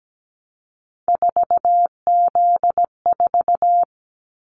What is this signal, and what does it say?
Morse code, heard as a keyed tone, about 17 wpm, 700 Hz. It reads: 4Z4
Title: Callsign Prefixes for Top 50 DXCC - 17wpm